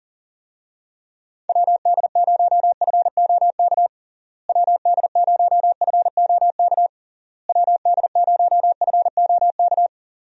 Morse code, heard as a keyed tone, about 37 wpm, 700 Hz. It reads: WD0FOK WD0FOK WD0FOK
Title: USA Call Signs - 6 Characters - Repeated 3x - 40wpm